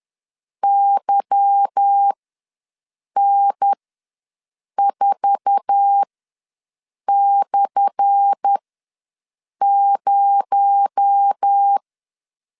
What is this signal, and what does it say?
Morse code, heard as a keyed tone, about 10 wpm, spaced as for 6 wpm, 800 Hz. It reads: YN4/0
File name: LEZIONI / 10 SIGNS / 10A SLASH ws 10wpm 4121